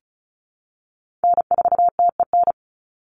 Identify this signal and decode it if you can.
Morse code, heard as a keyed tone, about 34 wpm, 700 Hz. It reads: N4TEN
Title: USA Call Signs 35wpm